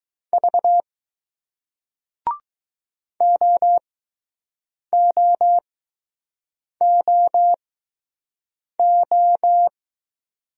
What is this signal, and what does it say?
Morse code, heard as a keyed tone, about 20 wpm, spaced as for 6 wpm, 700 Hz. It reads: V OOOO